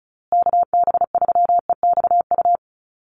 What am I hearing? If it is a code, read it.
Morse code, heard as a keyed tone, about 34 wpm, 700 Hz. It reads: KB3EXU
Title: USA Call Signs Repeated 2x 35wpm